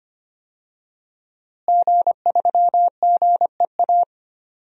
Morse code, heard as a keyed tone, about 25 wpm, 700 Hz. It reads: G3GEA
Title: DXCC-100 Call Signs - 5 Characters - Repeated 3x - 25wpm